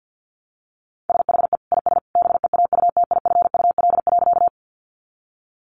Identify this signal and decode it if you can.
Morse code, heard as a keyed tone, about 50 wpm, 700 Hz. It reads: SHE IS BEAUTIFUL.
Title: Sentences from Top 900 Words 50wpm